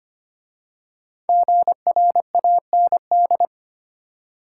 Morse code, heard as a keyed tone, about 25 wpm, 700 Hz. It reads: GRAND
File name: 5 Letter Words - First 3 Letters Given - 25wpm